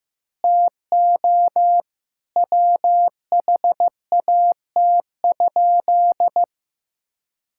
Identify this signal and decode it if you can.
Morse code, heard as a keyed tone, about 15 wpm, 700 Hz. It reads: TO WHAT?
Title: Sentences from Top 1000 Words - Review Part 6 - 15wpm